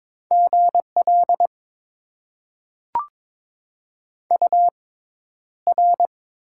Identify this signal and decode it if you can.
Morse code, heard as a keyed tone, about 22 wpm, spaced as for 11 wpm, 700 Hz. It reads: GL UR
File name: Top 90 QSO Elements - 6x Character Spacing - 22wpm